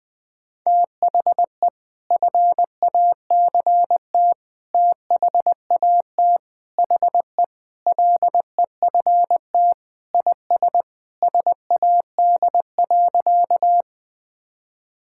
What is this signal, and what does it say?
Morse code, heard as a keyed tone, about 21 wpm, 700 Hz. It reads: THE FACT THAT HE LEFT IS SAD.
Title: Sentences with 2-Word Phrases - Spoken Beginning - 20wpm